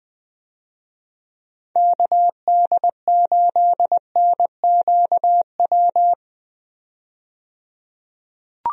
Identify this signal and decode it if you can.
Morse code, heard as a keyed tone, about 21 wpm, 700 Hz. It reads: KD8NQW
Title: USA Call Signs - 6 Characters - 20wpm